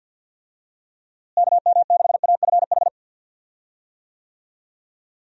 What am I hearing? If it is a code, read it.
Morse code, heard as a keyed tone, about 52 wpm, 700 Hz. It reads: KM6AFH
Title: USA Call Signs - 6 Characters - Repeated 2x - 50wpm